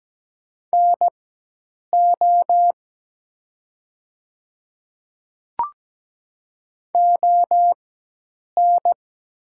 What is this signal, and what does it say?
Morse code, heard as a keyed tone, about 17 wpm, spaced as for 8 wpm, 700 Hz. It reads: NO ON
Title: 4x Character Spacing - Top 100 Words - 17wpm